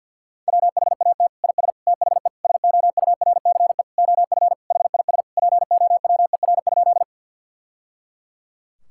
Morse code, heard as a keyed tone, about 50 wpm, 700 Hz. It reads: WHAT IS THE SOURCE OF HIS POWER?
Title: Sentences from Top 1000 Words - Review Part 5 - 50wpm